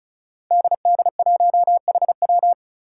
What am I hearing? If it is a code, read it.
Morse code, heard as a keyed tone, about 36 wpm, 700 Hz. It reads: DD1HW
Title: Five Challenging Letter-Number Combinations 35wpm